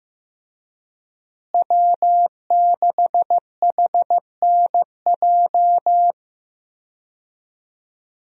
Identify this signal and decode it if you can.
Morse code, heard as a keyed tone, about 15 wpm, 700 Hz. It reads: W6HNJ
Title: USA Call Signs - 5 Characters - Repeated 3x - 15wpm